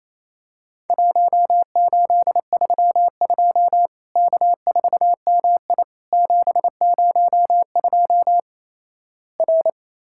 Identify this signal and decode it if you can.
Morse code, heard as a keyed tone, about 28 wpm, 700 Hz. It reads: 1832 K4MI 702 R
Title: Worked All Europe DX Contest - QTC Exchange - 28wpm